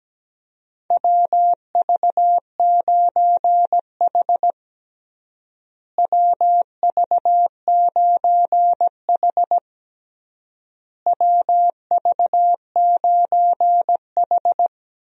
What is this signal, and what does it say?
Morse code, heard as a keyed tone, about 17 wpm, 700 Hz. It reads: WV9H WV9H WV9H
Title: USA Call Signs - 4 Characters - Repeated 3x - 17wpm